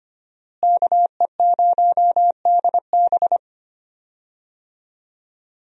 Morse code, heard as a keyed tone, about 25 wpm, 700 Hz. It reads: KE0DB
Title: USA Call Signs - 5 Characters - Repeated 3x - 25wpm